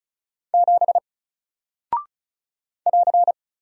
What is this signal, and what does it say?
Morse code, heard as a keyed tone, about 36 wpm, 700 Hz. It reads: 7 <AR>
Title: Single Character (Letter:Number:Punctuation:Prosign) Alt 35wpm